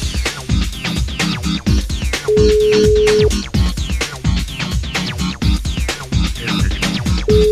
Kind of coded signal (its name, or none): telephone call progress tone